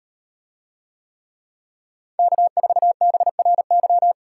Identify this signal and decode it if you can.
Morse code, heard as a keyed tone, about 38 wpm, 700 Hz. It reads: K4BRY